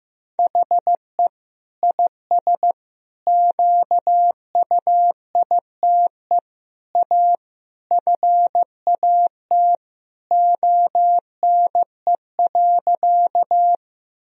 Morse code, heard as a keyed tone, about 16 wpm, 700 Hz. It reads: HE IS QUITE A FAT ONE.
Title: Sentences from Top 1000 Words - Review Part 4 - 15wpm